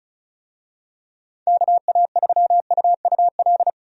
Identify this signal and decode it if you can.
Morse code, heard as a keyed tone, about 34 wpm, 700 Hz. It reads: KA3UUL
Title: USA Call Signs - 6 Characters - 35wpm